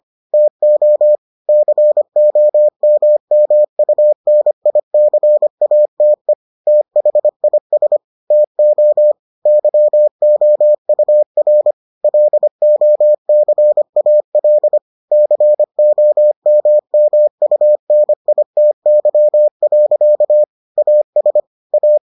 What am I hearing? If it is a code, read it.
Morse code, heard as a keyed tone, about 25 wpm, 600 Hz. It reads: TO COMMUNICATE THIS TO YOUR LOCAL COMMUNITY. AS A